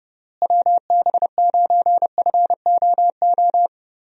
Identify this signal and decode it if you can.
Morse code, heard as a keyed tone, about 31 wpm, 700 Hz. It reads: WB9FOO
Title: USA Call Signs - 6 Characters - Repeated 3x - 30wpm